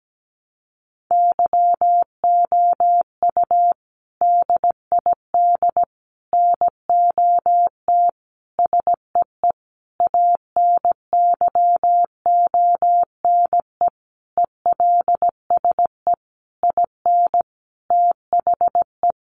Morse code, heard as a keyed tone, about 17 wpm, 700 Hz. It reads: YOU DID NOT SEE ANYONE ELSE IN THE
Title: Sentences from Top 900 Words 17wpm